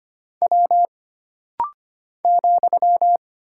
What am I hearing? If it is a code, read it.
Morse code, heard as a keyed tone, about 25 wpm, 700 Hz. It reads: W ,
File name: Single Character (Letter:Number:Punctuation:Prosign) - Rapid-Fire - 25wpm